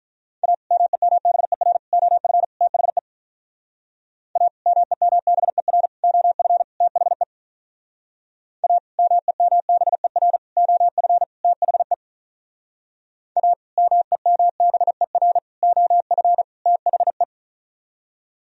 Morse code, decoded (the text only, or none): A MEMBER OF THE A MEMBER OF THE A MEMBER OF THE A MEMBER OF THE